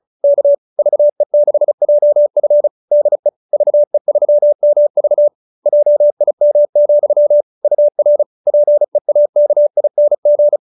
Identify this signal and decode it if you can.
Morse code, heard as a keyed tone, about 34 wpm, 590 Hz. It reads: K VE6JF DE VE3MV JIM, UR PEAKING